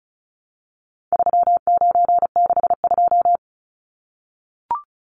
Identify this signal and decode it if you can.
Morse code, heard as a keyed tone, about 34 wpm, 700 Hz. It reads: 3962
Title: Numbers - 4-Digit 35wpm